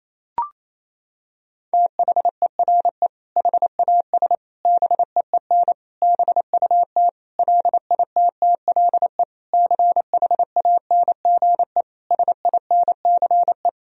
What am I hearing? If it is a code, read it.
Morse code, heard as a keyed tone, about 28 wpm, 700 Hz. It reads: THERE HAS BEEN BUT LITTLE CHANGE SINCE
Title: Sentences from Top 1000 Words - Review Part 1 - 28wpm